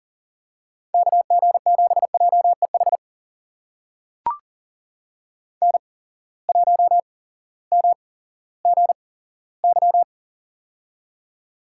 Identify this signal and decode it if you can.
Morse code, heard as a keyed tone, about 40 wpm, spaced as for 18 wpm, 700 Hz. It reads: KG7JEH N1MGY
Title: USA Call Signs - 8x Character Spacing - 40wpm